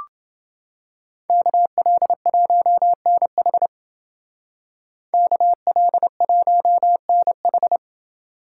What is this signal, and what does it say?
Morse code, heard as a keyed tone, about 28 wpm, 700 Hz. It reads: KL1NH KL1NH